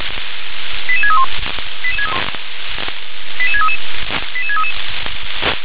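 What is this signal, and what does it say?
Five-tone selective call (ZVEI), heard as quick runs of five tones, tones of about 80 ms. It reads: 88631 88633 88633
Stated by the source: Received by SDR as ZVEI five-tone selective call.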